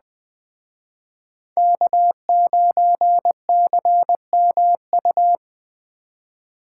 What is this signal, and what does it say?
Morse code, heard as a keyed tone, about 20 wpm, 700 Hz. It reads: K9CMU
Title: Speed Racing - USA Call Signs 17wpm